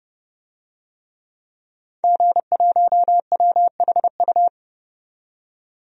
Morse code, heard as a keyed tone, about 30 wpm, 700 Hz. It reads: G1WHU
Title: DXCC-100 Call Signs - 5 Characters - 30wpm